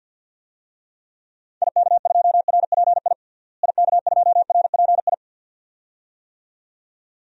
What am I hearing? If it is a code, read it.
Morse code, heard as a keyed tone, about 53 wpm, 700 Hz. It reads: IK2RPI IK2RPI
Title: DXCC-100 Call Signs - 6 Characters - Repeated 2x - 50wpm